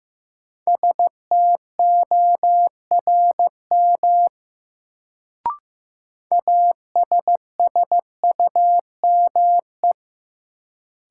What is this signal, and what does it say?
Morse code, heard as a keyed tone, about 15 wpm, 700 Hz. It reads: STORM ASSUME
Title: Sentences from Top 1500 Words - New Word Drill Alt - 15wpm